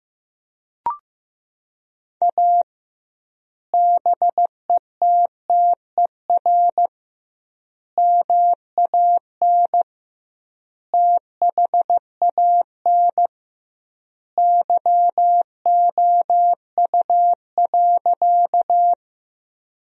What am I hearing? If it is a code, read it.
Morse code, heard as a keyed tone, about 15 wpm, 700 Hz. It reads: A BETTER MAN THAN YOU.
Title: 2x Word-Spacing - Sentences from Top 400 Words - 15wpm